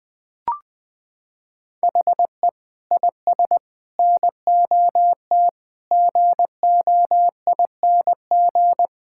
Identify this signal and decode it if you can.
Morse code, heard as a keyed tone, about 20 wpm, 700 Hz. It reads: HE IS NOT GOING
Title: Sentences with 4-Word Phrases - 20wpm